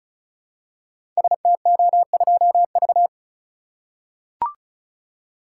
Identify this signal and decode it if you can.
Morse code, heard as a keyed tone, about 36 wpm, 700 Hz. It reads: STO2V
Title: Code Groups - 5 Letter:Numbers - 35wpm